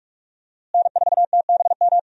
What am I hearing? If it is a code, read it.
Morse code, heard as a keyed tone, about 46 wpm, 700 Hz. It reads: N4TBM
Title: USA Call Signs - 5 Characters - Repeated 3x - 45wpm